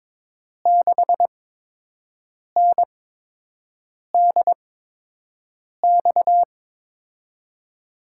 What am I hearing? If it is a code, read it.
Morse code, heard as a keyed tone, about 21 wpm, spaced as for 6 wpm, 700 Hz. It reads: 6NDX
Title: USA Call Signs - 8x Character Spacing - 22wpm